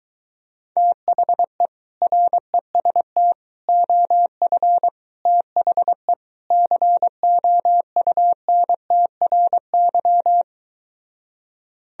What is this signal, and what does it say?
Morse code, heard as a keyed tone, about 23 wpm, 700 Hz. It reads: THE REST OF THE COUNTRY